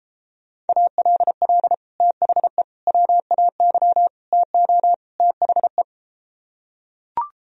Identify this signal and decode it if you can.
Morse code, heard as a keyed tone, about 33 wpm, 700 Hz. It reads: ALL THE WAY TO THE